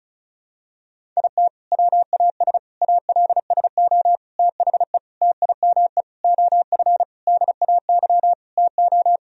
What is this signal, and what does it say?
Morse code, heard as a keyed tone, about 36 wpm, 700 Hz. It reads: IT WAS ALSO THE TIME OF DAY TO
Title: Sentences from Top 1000 Words - Review Part 1 - 35wpm